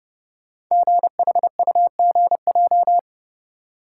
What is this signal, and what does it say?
Morse code, heard as a keyed tone, about 30 wpm, 700 Hz. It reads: GHUGJ
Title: Code Groups - 5 Letters - 30wpm